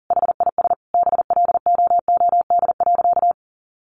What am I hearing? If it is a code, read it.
Morse code, heard as a keyed tone, about 40 wpm, 700 Hz. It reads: HIS BLOOD.
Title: Sentences from Top 1400 Words 40wpm